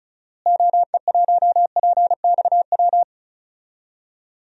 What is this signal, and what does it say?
Morse code, heard as a keyed tone, about 34 wpm, 700 Hz. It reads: OE1PXW
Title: DXCC-100 Call Signs - 6 Characters - Repeated 2x - 35wpm